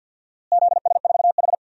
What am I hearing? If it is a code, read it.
Morse code, heard as a keyed tone, about 47 wpm, 700 Hz. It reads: ZS4H
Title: DXCC-100 Call Signs - 4 Characters - Repeated 3x - 50wpm